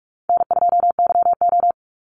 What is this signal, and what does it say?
Morse code, heard as a keyed tone, about 44 wpm, 700 Hz. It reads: N2YO
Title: Easy Call Signs Repeated 3x - 45wpm